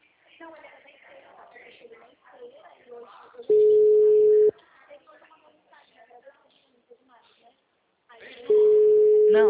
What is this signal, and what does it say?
Ringback tone (Europe, 425 Hz), with speech.
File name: out- -2025- - - .78